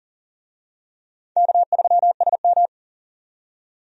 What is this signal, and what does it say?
Morse code, heard as a keyed tone, about 40 wpm, 700 Hz. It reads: K3SM
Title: USA Call Signs - 4 Characters - 40wpm